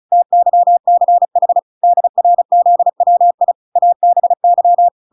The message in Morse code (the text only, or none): TYCH DRZWI ABY